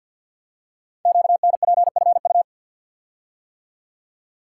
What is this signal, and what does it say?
Morse code, heard as a keyed tone, about 51 wpm, 700 Hz. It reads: QNPFU
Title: Code Groups - 5 Letters - 50wpm